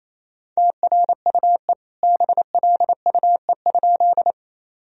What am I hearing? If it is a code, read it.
Morse code, heard as a keyed tone, about 28 wpm, 700 Hz. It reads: TRUE BLUE?
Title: Sentences from Top 1000 Words - Review Part 3 - 28wpm